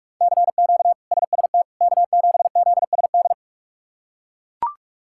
Morse code, heard as a keyed tone, about 42 wpm, 700 Hz. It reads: CQ SST K7ZSD
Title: K1USN SST QSO Practice - 45wpm